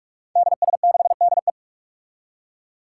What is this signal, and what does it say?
Morse code, heard as a keyed tone, about 46 wpm, 700 Hz. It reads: DS6DE
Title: Five Challenging Letter-Number Combinations 45wpm